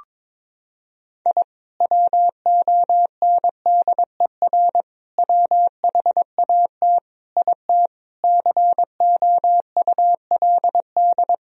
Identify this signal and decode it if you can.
Morse code, heard as a keyed tone, about 21 wpm, 700 Hz. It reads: I WONDER WHAT IT COULD
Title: Sentences from Top 1000 Words - Review Part 3 - 22wpm